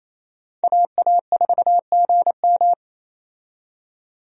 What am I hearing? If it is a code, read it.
Morse code, heard as a keyed tone, about 27 wpm, 700 Hz. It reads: AA4GM
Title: USA Call Signs - 5 Characters - 28wpm